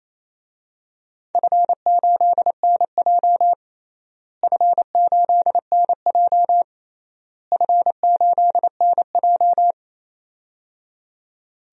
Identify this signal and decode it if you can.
Morse code, heard as a keyed tone, about 28 wpm, 700 Hz. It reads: F8NJ F8NJ F8NJ
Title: DXCC-100 Call Signs - 4 Characters - Repeated 3x - 28wpm